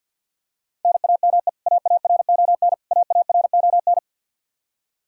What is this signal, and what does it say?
Morse code, heard as a keyed tone, about 52 wpm, 700 Hz. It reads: NAME AARON AARON
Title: Practice QSO - Exchange 1 - Beginning - 50wpm